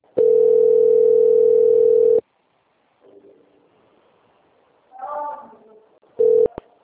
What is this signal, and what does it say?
Ringback tone (North America), 440 + 480 Hz.